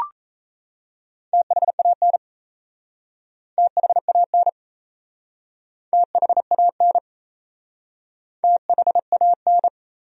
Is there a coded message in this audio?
Yes: Morse code from 1.3 to 9.7 s, heard as a keyed tone, about 38 wpm, 700 Hz. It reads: THAN THAN THAN THAN